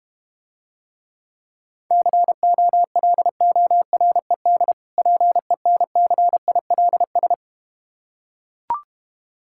Morse code, heard as a keyed tone, about 30 wpm, 700 Hz. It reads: COLORED PENCILS
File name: Easy Words - Speed-Racing with Hints - 25wpm